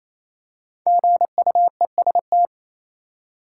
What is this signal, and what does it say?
Morse code, heard as a keyed tone, about 28 wpm, 700 Hz. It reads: GUEST
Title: Five Letter Words - 28wpm